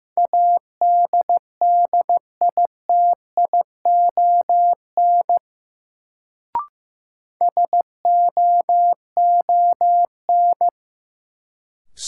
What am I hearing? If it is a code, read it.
Morse code, heard as a keyed tone, about 15 wpm, 700 Hz. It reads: ADDITION SOON
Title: Sets of 3 Words - New Word Drill - 15wpm